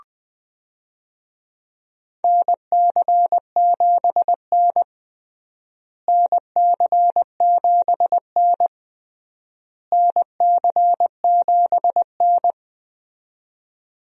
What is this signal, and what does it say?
Morse code, heard as a keyed tone, about 21 wpm, 700 Hz. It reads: NC7N NC7N NC7N
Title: USA Call Signs - 4 Characters - Repeated 3x - 20wpm